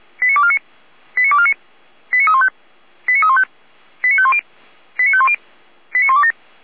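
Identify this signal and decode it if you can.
Five-tone selective call (ZVEI), heard as quick runs of five tones, tones of about 70 ms. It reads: 89249 89249 89315 89315 89510 89510 89127
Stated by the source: ZVEI five-tone call 89250 received by SDR, channel 455, recorded by feuerwehr-orlamuende fire brigade server.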